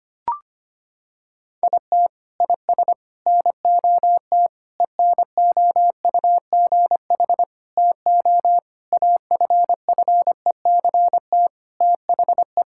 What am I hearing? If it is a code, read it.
Morse code, heard as a keyed tone, about 25 wpm, 700 Hz. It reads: IT IS NOT ENOUGH TO AFFECT THE
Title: Sentences from Top 1000 Words - Review Part 6 - 25wpm